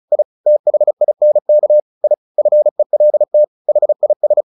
Morse code, heard as a keyed tone, about 34 wpm, 600 Hz. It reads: I THINK I FELT HIS
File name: the old man and the sea 2 35wpm